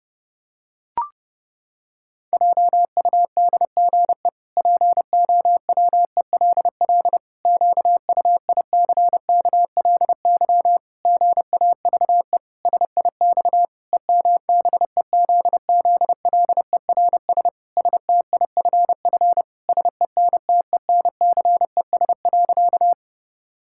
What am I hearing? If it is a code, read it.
Morse code, heard as a keyed tone, about 30 wpm, 700 Hz. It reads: JUDGE POWELL QUICKLY GAVE SIX EMBEZZLERS STIFF SENTENCES.
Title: Pangram Challenge v2 - 30wpm